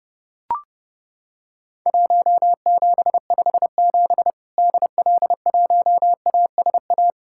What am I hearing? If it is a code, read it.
Morse code, heard as a keyed tone, about 30 wpm, 700 Hz. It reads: 1757 DL1ASA
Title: Worked All Europe DX Contest - QTC Exchange - 30wpm